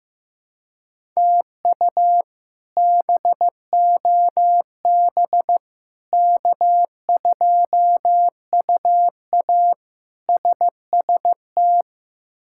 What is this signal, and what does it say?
Morse code, heard as a keyed tone, about 15 wpm, 700 Hz. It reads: TU BOB K2UA SST
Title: K1USN SST QSO Practice - 15wpm